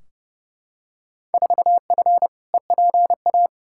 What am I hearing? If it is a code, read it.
Morse code, heard as a keyed tone, about 32 wpm, 700 Hz. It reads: 4F EPA
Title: Field Day Exchanges Encore - 30wpm